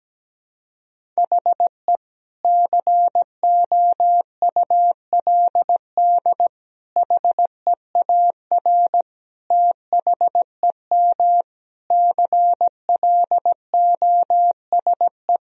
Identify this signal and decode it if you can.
Morse code, heard as a keyed tone, about 17 wpm, 700 Hz. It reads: HE COULD HEAR THEM CLOSE
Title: Sentences from Top 1000 Words - Review Part 1 - 17wpm